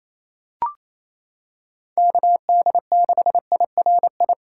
Morse code, heard as a keyed tone, about 28 wpm, 700 Hz. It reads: KD6IRI
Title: K1USN SST QSO Practice - 28wpm